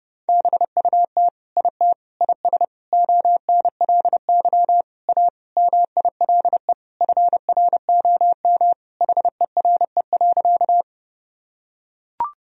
Morse code, heard as a keyed tone, about 30 wpm, 700 Hz. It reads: BUT IT IS ONLY A MILE FROM HERE.
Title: Sentences from Top 1000 Words - Review Part 1 - 30wpm